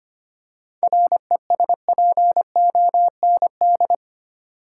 Morse code, heard as a keyed tone, about 25 wpm, 700 Hz. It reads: RESPOND
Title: Seven Letter Words - 25wpm